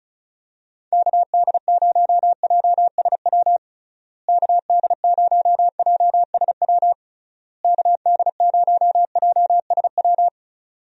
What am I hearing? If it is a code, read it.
Morse code, heard as a keyed tone, about 34 wpm, 700 Hz. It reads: KD0JSW KD0JSW KD0JSW
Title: USA Call Signs - 6 Characters - Repeated 3x - 35wpm